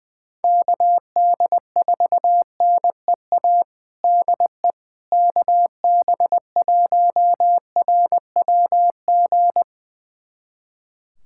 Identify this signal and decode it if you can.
Morse code, heard as a keyed tone, about 21 wpm, 700 Hz. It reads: KD4NEA DE KB1RWG
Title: USA Callsign DE Callsign - 6 Characters - 20wpm